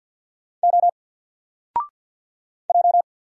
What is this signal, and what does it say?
Morse code, heard as a keyed tone, about 50 wpm, 700 Hz. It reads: O J
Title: Single Character (Letter:Number:Punctuation:Prosign) 50wpm